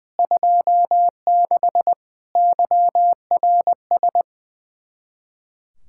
Morse code, heard as a keyed tone, about 21 wpm, 700 Hz. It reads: 26 YRS
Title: Speed-Racing Common QSO Phrases - 20wpm